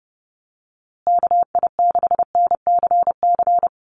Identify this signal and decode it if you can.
Morse code, heard as a keyed tone, about 30 wpm, 700 Hz. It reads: KI6NCC
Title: USA Call Signs Repeated 3x 30wpm